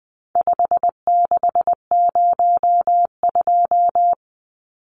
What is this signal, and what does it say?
Morse code, heard as a keyed tone, about 19 wpm, 700 Hz. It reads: 5602
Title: Numbers - 4-Digit 20wpm